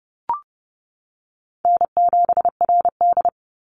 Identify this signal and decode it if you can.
Morse code, heard as a keyed tone, about 32 wpm, 700 Hz. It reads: N7RD
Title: Easy Call Signs - 30wpm